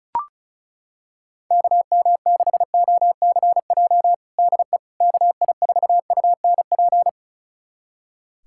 Morse code, heard as a keyed tone, about 34 wpm, 700 Hz. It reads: KM6OCJ DE KI4UNP
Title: USA Callsign DE Callsign - 6 Characters - 35wpm